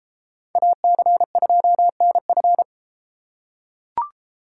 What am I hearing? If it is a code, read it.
Morse code, heard as a keyed tone, about 33 wpm, 700 Hz. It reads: AC2NF